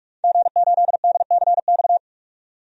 Morse code, heard as a keyed tone, about 46 wpm, 700 Hz. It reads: G8DCX
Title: DXCC-100 Call Signs - 5 Characters - 45wpm